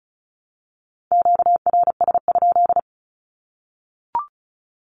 Morse code, heard as a keyed tone, about 34 wpm, 700 Hz. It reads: QRS?
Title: Q-Codes 35wpm